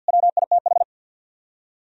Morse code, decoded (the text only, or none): WITH